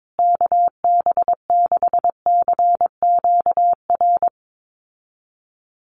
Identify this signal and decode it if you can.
Morse code, heard as a keyed tone, about 22 wpm, 700 Hz. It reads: KB6CQR
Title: USA Call Signs Repeated 3x 22wpm